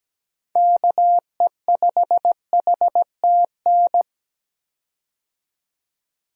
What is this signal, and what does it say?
Morse code, heard as a keyed tone, about 17 wpm, 700 Hz. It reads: KE5HTN
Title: USA Call Signs - 6 Characters - 17wpm